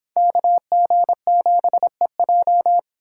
Morse code, heard as a keyed tone, about 31 wpm, 700 Hz. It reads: KG7EJ